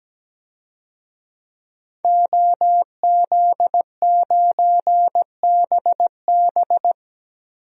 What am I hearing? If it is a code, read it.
Morse code, heard as a keyed tone, about 17 wpm, 700 Hz. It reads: OZ9BB
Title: DXCC-100 Call Signs - 5 Characters - 17wpm